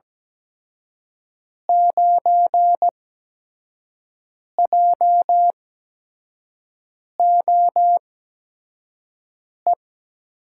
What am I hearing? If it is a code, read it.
Morse code, heard as a keyed tone, about 17 wpm, spaced as for 5 wpm, 700 Hz. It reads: 9JOE
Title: USA Call Signs - 6 Characters - 8x Character Spacing - 17wpm